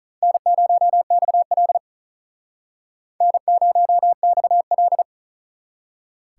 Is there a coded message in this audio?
Yes: Morse code from 0.2 to 5.0 s, heard as a keyed tone, about 44 wpm, 700 Hz. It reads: N0XL N0XL